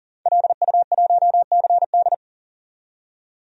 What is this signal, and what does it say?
Morse code, heard as a keyed tone, about 38 wpm, 700 Hz. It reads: LU1CD
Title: DXCC-100 Call Signs - 5 Characters - 40wpm